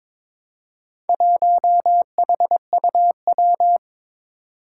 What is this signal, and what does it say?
Morse code, heard as a keyed tone, about 21 wpm, 700 Hz. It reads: 1HUW
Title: Four Challenging Letter-Number Combinations 22wpm